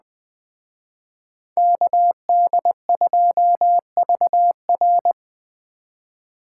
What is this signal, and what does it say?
Morse code, heard as a keyed tone, about 20 wpm, 700 Hz. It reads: KD2VR